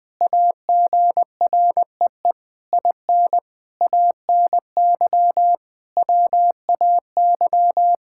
Morse code, heard as a keyed tone, about 20 wpm, 700 Hz. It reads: AGREE IN ANY WAY
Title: Sentences with 6-Word Phrases - Spoken Beginning - 20wpm